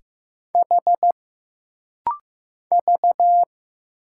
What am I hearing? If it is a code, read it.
Morse code, heard as a keyed tone, about 15 wpm, 700 Hz. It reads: H V
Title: Single Letters - Rapid-Fire 15wpm